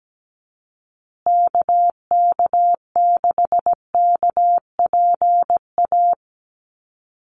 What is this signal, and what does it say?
Morse code, heard as a keyed tone, about 17 wpm, 700 Hz. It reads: KK6KPA
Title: USA Call Signs Repeated 3x 17wpm